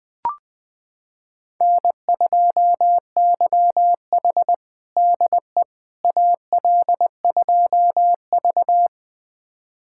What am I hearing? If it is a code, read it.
Morse code, heard as a keyed tone, about 20 wpm, 700 Hz. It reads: N2YH DE AL2V
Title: USA Callsign DE Callsign - 4 Characters - 20wpm